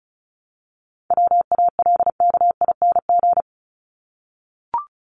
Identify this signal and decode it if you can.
Morse code, heard as a keyed tone, about 34 wpm, 700 Hz. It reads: WALKING
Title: ING Suffix - Top 100 Words with Base Words 35wpm